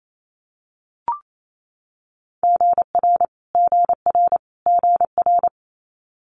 Morse code, heard as a keyed tone, about 28 wpm, 700 Hz. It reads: GR GR GR
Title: Multi-letter Phonemes Repeated 3x 28wpm